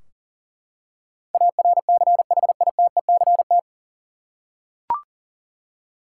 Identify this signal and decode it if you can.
Morse code, heard as a keyed tone, about 40 wpm, 700 Hz. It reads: ARCHITECT
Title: Practice QSO - Exchange 3 - More Info - 40wpm